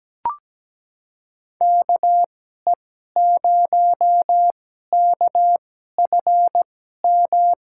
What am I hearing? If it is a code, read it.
Morse code, heard as a keyed tone, about 17 wpm, spaced as for 12 wpm, 700 Hz. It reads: KE0KFM
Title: USA Call Signs - 6 Characters - 2x Character Spacing - 17wpm